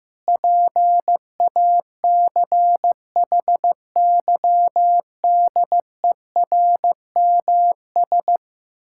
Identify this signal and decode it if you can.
Morse code, heard as a keyed tone, about 15 wpm, 700 Hz. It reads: PACHYDERMS
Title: Pangram Challenge v2 - 15wpm